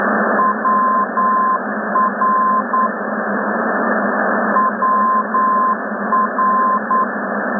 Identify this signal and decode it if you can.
Morse code, heard as a keyed tone, about 9 wpm, 1040 Hz, with noise. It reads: WR WR
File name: rsp2 ndb